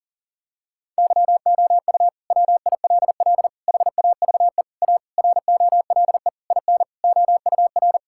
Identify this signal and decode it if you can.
Morse code, heard as a keyed tone, about 37 wpm, 700 Hz. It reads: YOU WILL HAVE A ROLE IN OUR
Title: Sentences from Top 1000 Words - Review Part 6 - 40wpm